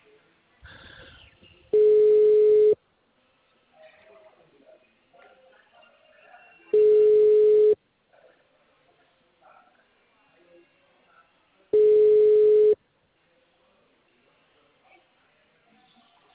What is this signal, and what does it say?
Ringback tone (Europe, 425 Hz), with speech.